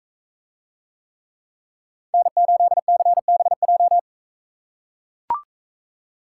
Morse code, heard as a keyed tone, about 44 wpm, 700 Hz. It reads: N8CBJ